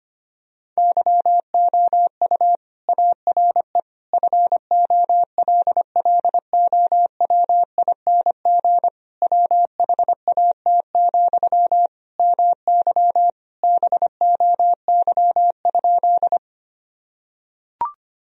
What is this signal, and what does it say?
Morse code, heard as a keyed tone, about 25 wpm, 700 Hz. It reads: YOU ARE FOLLOWING WHAT, MY BOY?
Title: Sentences from Top 1000 Words - Review Part 4 - 25wpm